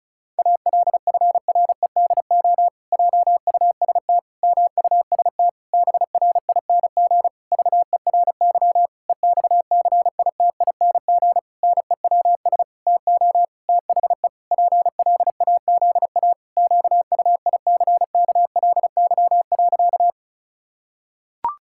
Morse code, heard as a keyed tone, about 36 wpm, 700 Hz. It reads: ALFREDO JUST MUST BRING VERY EXCITING NEWS TO THE PLAZA QUICKLY.
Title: Pangram Challenge v2 - 35wpm